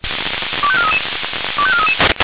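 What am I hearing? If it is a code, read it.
Five-tone selective call (ZVEI), heard as quick runs of five tones, tones of about 70 ms. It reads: 25433 25433
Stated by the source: Received by SDR; selective call numbers in /496/ folder.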